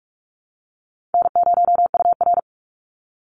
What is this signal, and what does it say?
Morse code, heard as a keyed tone, about 44 wpm, 700 Hz. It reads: N0UR
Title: Call Signs Repeated 3x - 45wpm